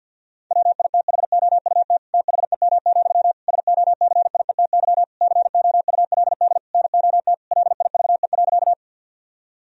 Morse code, heard as a keyed tone, about 50 wpm, 700 Hz. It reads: WITHOUT THEM, SOCIETY COULD NOT LIVE.
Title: Sentences from Top 1000 Words - Review Part 4 - 50wpm